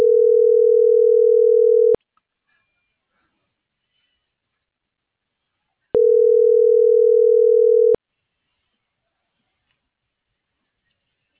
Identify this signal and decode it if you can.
Ringback tone (North America), 440 + 480 Hz.